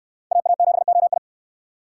Morse code, heard as a keyed tone, about 63 wpm, 700 Hz. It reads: AA7OI